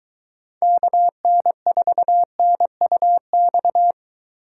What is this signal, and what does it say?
Morse code, heard as a keyed tone, about 22 wpm, 700 Hz. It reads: KN4NUX